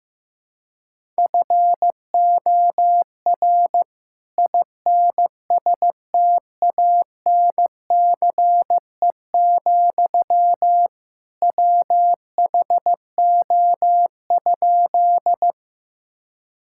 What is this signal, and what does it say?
Morse code, heard as a keyed tone, about 14 wpm, 700 Hz. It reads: FOR INSTANCE, WHO?
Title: Sentences from Top 1000 Words - Review Part 5 - 15wpm